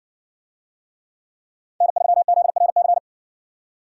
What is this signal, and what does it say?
Morse code, heard as a keyed tone, about 60 wpm, 700 Hz. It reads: N3ZRC